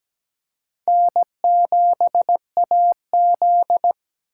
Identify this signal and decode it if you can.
Morse code, heard as a keyed tone, about 17 wpm, 700 Hz. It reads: N7AZ
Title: K1USN SST QSO Practice - 17wpm